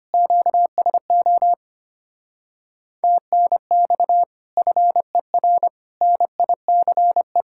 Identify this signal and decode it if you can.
Morse code, heard as a keyed tone, about 32 wpm, 700 Hz. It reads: QSO TNX FER NICE